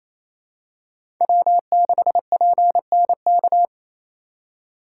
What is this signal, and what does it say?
Morse code, heard as a keyed tone, about 28 wpm, 700 Hz. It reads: W6PNK
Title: USA Call Signs - 3x Character Spacing - 28wpm